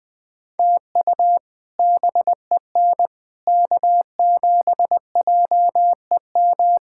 Morse code, heard as a keyed tone, about 20 wpm, 700 Hz. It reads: TU BEN K7JEM
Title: K1USN SST QSO Practice - 20wpm